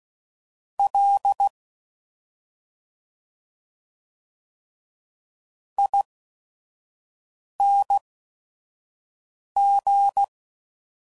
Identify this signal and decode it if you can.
Morse code, heard as a keyed tone, about 16 wpm, spaced as for 5 wpm, 800 Hz. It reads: L ING
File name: Suffixes / Special ING's / Sngl ING's